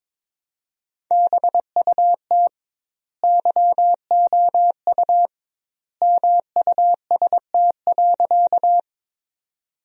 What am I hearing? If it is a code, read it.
Morse code, heard as a keyed tone, about 22 wpm, 700 Hz. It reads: BUT YOU MUST.
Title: 2x Word-Spacing - Sentences from Top 200 Words - 22wpm